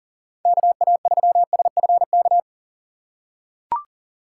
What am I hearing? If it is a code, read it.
Morse code, heard as a keyed tone, about 40 wpm, 700 Hz. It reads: KA3SFK
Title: USA Call Signs - 4x Character Spacing - 40wpm